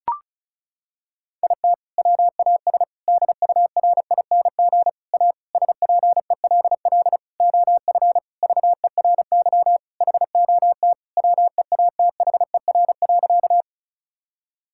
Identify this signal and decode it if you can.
Morse code, heard as a keyed tone, about 34 wpm, 700 Hz. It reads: IT WAS DURING A SPELL OF VERY HOT WEATHER.
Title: Sentences from Top 1000 Words - Review Part 5 - 35wpm